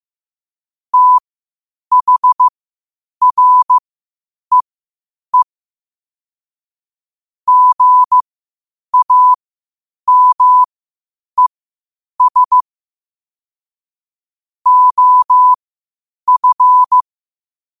Morse code, heard as a keyed tone, about 15 wpm, spaced as for 8 wpm, 1000 Hz. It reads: THREE GAMES OF
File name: qotdfiles / 8wpm